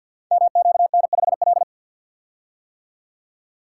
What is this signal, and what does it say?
Morse code, heard as a keyed tone, about 52 wpm, 700 Hz. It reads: MQN5L
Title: Code Groups - 5 Letter:Numbers - 50wpm